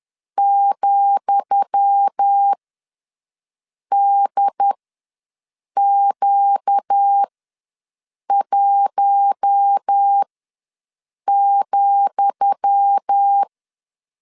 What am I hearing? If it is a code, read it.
Morse code, heard as a keyed tone, about 10 wpm, spaced as for 6 wpm, 800 Hz. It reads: ,DQ1,
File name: LEZIONI / 10 SIGNS / 10D COMA ws 10wpm 4314